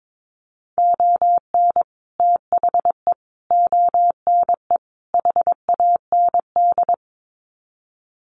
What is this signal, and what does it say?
Morse code, heard as a keyed tone, about 22 wpm, 700 Hz. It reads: ON THE ONE HAND
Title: Sets of 4 Words 22wpm